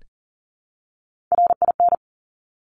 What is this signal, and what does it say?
Morse code, heard as a keyed tone, about 42 wpm, 700 Hz. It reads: RIN
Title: Top Three Letter Combinations in Words 40wpm